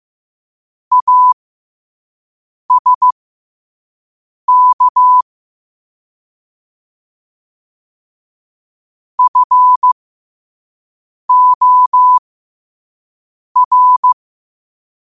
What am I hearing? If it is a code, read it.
Morse code, heard as a keyed tone, about 15 wpm, spaced as for 5 wpm, 1000 Hz. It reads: ASK FOR
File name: 5wpm